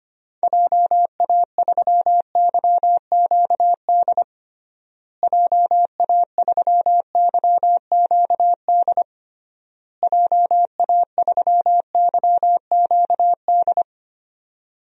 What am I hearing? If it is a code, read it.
Morse code, heard as a keyed tone, about 25 wpm, 700 Hz. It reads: JA3YQD JA3YQD JA3YQD
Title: DXCC-100 Call Signs - 6 Characters - Repeated 3x - 25wpm